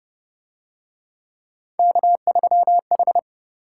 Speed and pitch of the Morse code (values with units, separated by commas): 32 wpm, 700 Hz